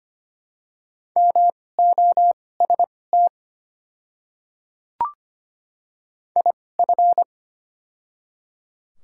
Morse code, heard as a keyed tone, about 25 wpm, spaced as for 18 wpm, 700 Hz. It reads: MOST IF
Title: 2x Character Spacing - Top 300 Words - 25wpm